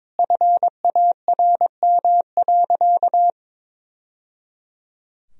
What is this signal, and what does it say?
Morse code, heard as a keyed tone, about 21 wpm, 700 Hz. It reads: FARM.
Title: Sentences from Top 1000 Words - Review Part 2 - 22wpm